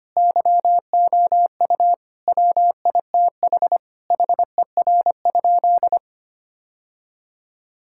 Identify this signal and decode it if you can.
Morse code, heard as a keyed tone, about 25 wpm, 700 Hz. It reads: YOU WITH HER?
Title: Sentences from Top 1000 Words - Review Part 5 - 25wpm